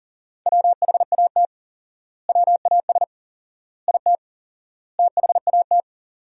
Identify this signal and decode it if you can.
Morse code, heard as a keyed tone, about 40 wpm, 700 Hz. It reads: WHAT WAS IT THAT
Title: 4x Word-Spacing - Sentences from Top 300 Words - 40wpm